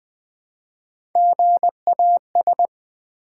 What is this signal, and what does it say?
Morse code, heard as a keyed tone, about 20 wpm, 700 Hz. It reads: GAS
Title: Build A Word - 1 Letter At A Time - 3-Letter Words - 20wpm